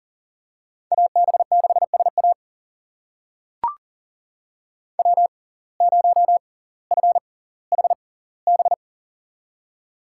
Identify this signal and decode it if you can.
Morse code, heard as a keyed tone, about 40 wpm, spaced as for 20 wpm, 700 Hz. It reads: AB6SA W0FHB
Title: USA Call Signs - 5 Characters - 6x Character Spacing - 40wpm